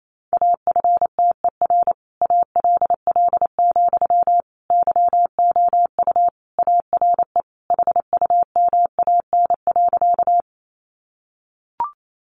Morse code, heard as a keyed tone, about 28 wpm, 700 Hz. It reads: AFTER ALL, YOU ARE HUMAN.
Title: Sentences from Top 600 Words 28wpm